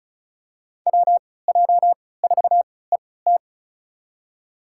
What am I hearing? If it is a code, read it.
Morse code, heard as a keyed tone, about 36 wpm, spaced as for 20 wpm, 690 Hz. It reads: WJ4ET
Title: USA Call Signs - 3x Character Spacing - 35wpm